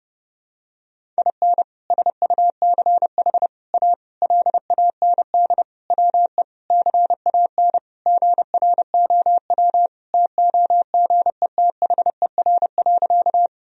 Morse code, heard as a keyed tone, about 32 wpm, 700 Hz. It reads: IN SUCH A LAND WE CAN GROW TOGETHER.
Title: Sentences from Top 1000 Words - Review Part 1 - 30wpm